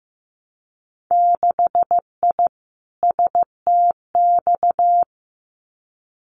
Morse code, heard as a keyed tone, about 15 wpm, 700 Hz. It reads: 6I STX
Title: Winter Field Day Exchanges Encore 15wpm